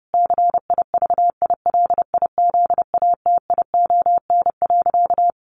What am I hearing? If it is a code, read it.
Morse code, heard as a keyed tone, about 30 wpm, 700 Hz. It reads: CIVILIZATION.
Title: A Sound Of Thunder 30wpm